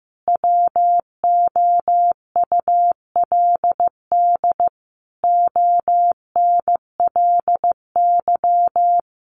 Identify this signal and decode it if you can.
Morse code, heard as a keyed tone, about 15 wpm, 700 Hz. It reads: WOULD ONLY
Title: Sentences from Top 1500 Words 15wpm